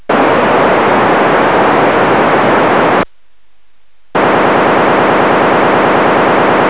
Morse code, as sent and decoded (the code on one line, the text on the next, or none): none